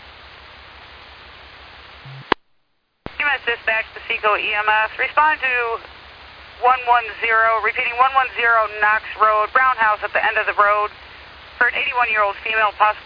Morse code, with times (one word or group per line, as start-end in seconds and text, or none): none